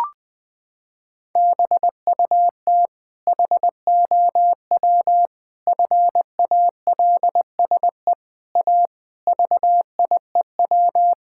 Morse code, heard as a keyed tone, about 20 wpm, 700 Hz. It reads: BUT HOW FALSE A VIEW
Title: Sentences from Top 1600 Words 20wpm